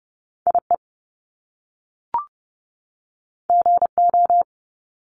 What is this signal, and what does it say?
Morse code, heard as a keyed tone, about 30 wpm, 700 Hz. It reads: IE GO
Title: Top Two Letter Combinations in Words 30wpm